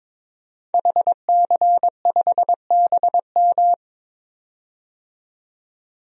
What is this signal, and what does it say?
Morse code, heard as a keyed tone, about 22 wpm, 700 Hz. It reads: HC5BM
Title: DXCC-100 Call Signs - 5 Characters - 22wpm